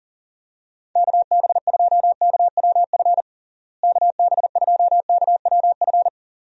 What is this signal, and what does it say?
Morse code, heard as a keyed tone, about 37 wpm, 700 Hz. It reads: KB2KWF KB2KWF
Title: USA Call Signs - 6 Characters - Repeated 2x - 40wpm